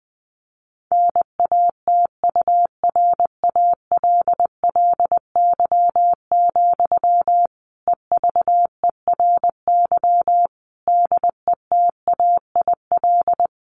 Morse code, heard as a keyed tone, about 20 wpm, 700 Hz. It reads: NATURALLY, EVERY DETAIL
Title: Sentences from Top 1500 Words 20wpm